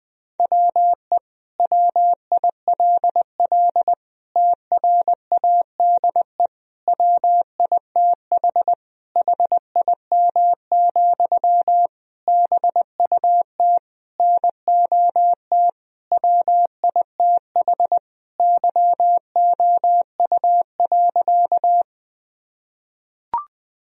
Morse code, heard as a keyed tone, about 19 wpm, 700 Hz. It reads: WE WILL TRADE WITH HIM, BUT NOT WITH YOU.
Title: Sentences from Top 1000 Words - Review Part 4 - 20wpm